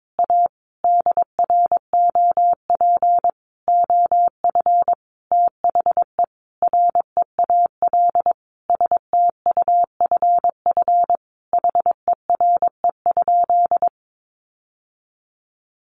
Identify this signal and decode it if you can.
Morse code, heard as a keyed tone, about 22 wpm, 700 Hz. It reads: A DROP OF THE REAL STUFF HERE?
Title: Sentences from Top 1400 Words 22wpm